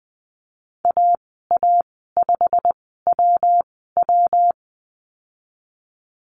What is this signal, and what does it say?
Morse code, heard as a keyed tone, about 20 wpm, spaced as for 15 wpm, 700 Hz. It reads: AA5WW
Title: 128 Lesson 21g - Number 6 - Call Signs 20wpm